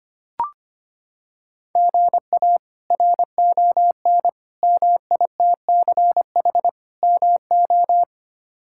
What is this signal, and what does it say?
Morse code, heard as a keyed tone, about 25 wpm, 700 Hz. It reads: GA RON MITCH MO
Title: K1USN SST QSO Practice - 25wpm